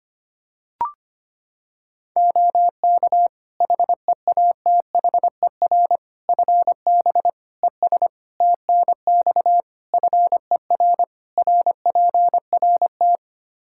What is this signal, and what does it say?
Morse code, heard as a keyed tone, about 25 wpm, 700 Hz. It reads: OK HEATHER FB ES TNX FER RPRT
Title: K1USN SST QSO Practice 25wpm